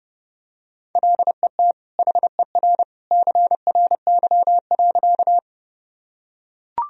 Morse code, heard as a keyed tone, about 30 wpm, 700 Hz. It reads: LET HER CRY.
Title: Sentences from Top 1000 Words - Review Part 2 - 30wpm